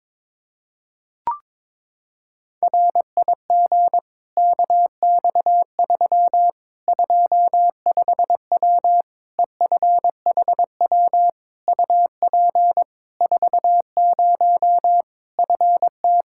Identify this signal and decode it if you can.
Morse code, heard as a keyed tone, about 22 wpm, 700 Hz. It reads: RIG KX3 25W EFHW UP 40 FT
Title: Practice QSO - Exchange 2 - Info - 22wpm